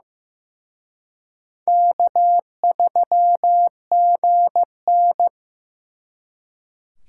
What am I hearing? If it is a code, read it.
Morse code, heard as a keyed tone, about 18 wpm, 700 Hz. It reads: K3GN
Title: Speed Racing - USA 1x2 Call Signs 15wpm